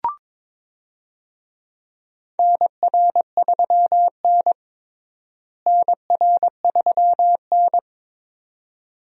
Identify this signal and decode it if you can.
Morse code, heard as a keyed tone, about 21 wpm, 700 Hz. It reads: NR3N NR3N
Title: USA Call Signs - 4 Characters - Repeated 2x - 22wpm